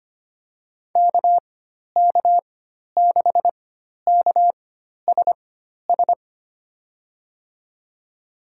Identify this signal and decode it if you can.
Morse code, heard as a keyed tone, about 25 wpm, spaced as for 12 wpm, 700 Hz. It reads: KK6KSS
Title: USA Call Signs - 4x Character Spacing - 25wpm